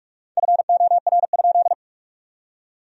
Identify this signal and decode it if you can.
Morse code, heard as a keyed tone, about 46 wpm, 700 Hz. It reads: FOR?
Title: 4x Word-Spacing - Sentences from Top 200 Words - 45wpm